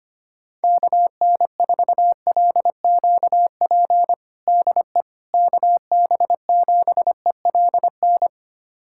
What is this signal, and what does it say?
Morse code, heard as a keyed tone, about 25 wpm, 700 Hz. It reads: KN4LQP DE KB7ELN
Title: USA Callsign DE Callsign 25wpm